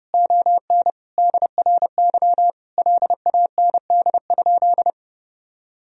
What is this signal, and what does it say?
Morse code, heard as a keyed tone, about 32 wpm, 700 Hz. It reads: ON DRY LAND?
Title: Sentences from Top 1000 Words - Review Part 3 - 30wpm